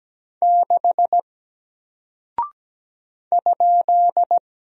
Morse code, heard as a keyed tone, about 17 wpm, 700 Hz. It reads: 6 ?
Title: Single Character (Letter:Number:Punctuation:Prosign) Alt 17wpm